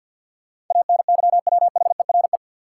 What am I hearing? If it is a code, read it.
Morse code, heard as a keyed tone, about 52 wpm, 700 Hz. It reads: ANYWHERE
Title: Sentences from Top 1400 Words - New Word Drill - 50wpm